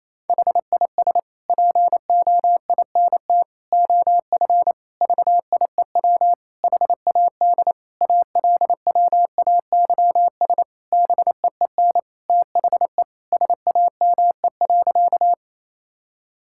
Morse code, heard as a keyed tone, about 28 wpm, 700 Hz. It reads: HIS POINT OF VIEW HAD ALWAYS BEEN THE SAME.
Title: Sentences from Top 1000 Words - Review Part 6 - 28wpm